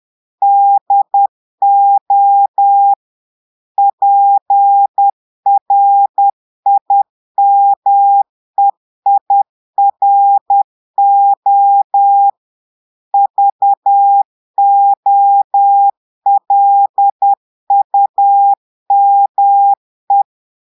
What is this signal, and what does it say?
Morse code, heard as a keyed tone, about 10 wpm, 800 Hz. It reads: DO PRIMEIRO VOLUME
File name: Portugais - texte en clair 1 - vitesse - 10 mots minute